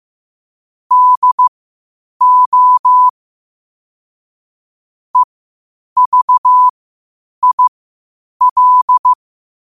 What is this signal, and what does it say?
Morse code, heard as a keyed tone, about 15 wpm, spaced as for 8 wpm, 1000 Hz. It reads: DO EVIL
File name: qotdfiles / 8wpm